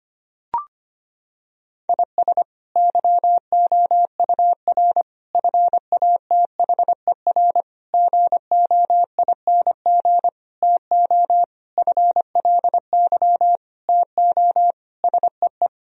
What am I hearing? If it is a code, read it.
Morse code, heard as a keyed tone, about 25 wpm, 700 Hz. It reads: IS YOUR FATHER GOING TO FLY TO SEE
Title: Sentences from Top 1000 Words - Review Part 2 - 25wpm